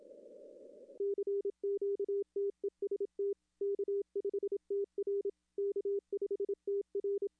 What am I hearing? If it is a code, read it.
Morse code, heard as a keyed tone, about 26 wpm, 410 Hz. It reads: CQTEST K5TR K5TR
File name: contests / arrl-160-k5tr-op-w5tn -0400